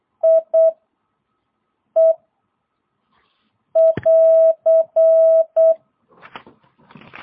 Morse code, heard as a keyed tone, about 8 wpm, 660 Hz. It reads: I E <AR>
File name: jh8cbh / 2022cw